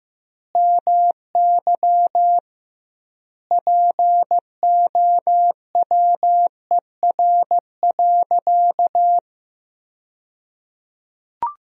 Morse code, heard as a keyed tone, about 16 wpm, 700 Hz. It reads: MY POWER.
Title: 2x Word-Spacing - Sentences from Top 500 Words - 15wpm